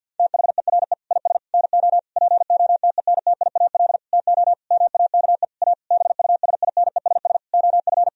Morse code, heard as a keyed tone, about 49 wpm, 700 Hz. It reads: THERE IS NO POTENTIAL TO MAKE A BUSINESS OF
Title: Sentences from Top 1000 Words - Review Part 4 - 50wpm